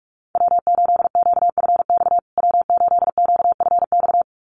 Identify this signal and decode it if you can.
Morse code, heard as a keyed tone, about 44 wpm, 690 Hz. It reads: W8QFX W8QFX
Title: USA Call Signs Repeated 2x 45wpm